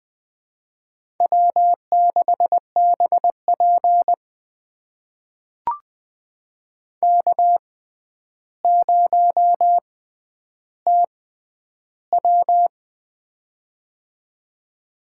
Morse code, heard as a keyed tone, about 20 wpm, spaced as for 10 wpm, 700 Hz. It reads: W6BP K0TW
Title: USA Call Signs - 4 Characters - 6x Character Spacing - 20wpm